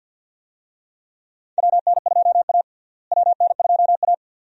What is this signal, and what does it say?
Morse code, heard as a keyed tone, about 52 wpm, 700 Hz. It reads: WN2A WN2A
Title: USA Call Signs - 4 Characters - Repeated 2x - 50wpm